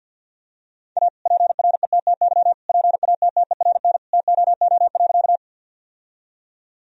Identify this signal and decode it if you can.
Morse code, heard as a keyed tone, about 50 wpm, 700 Hz. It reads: A PRETTY PATTERN TOO.
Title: Sentences from Top 1000 Words - Review Part 2 - 50wpm